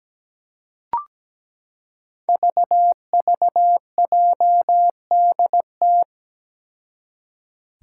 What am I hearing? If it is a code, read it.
Morse code, heard as a keyed tone, about 17 wpm, 700 Hz. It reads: VVJDT
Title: Five Challenging Letter-Number Combinations 17wpm